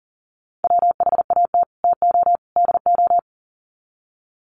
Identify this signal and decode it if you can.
Morse code, heard as a keyed tone, about 40 wpm, 700 Hz. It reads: WHAT TO DO
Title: Sets of 3 Words 40wpm